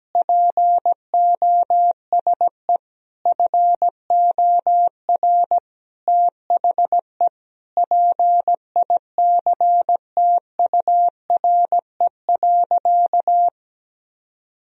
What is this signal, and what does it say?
Morse code, heard as a keyed tone, about 17 wpm, 700 Hz. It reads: POSE FOR THE PICTURE.
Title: Sentences from Top 1000 Words - Review Part 2 - 17wpm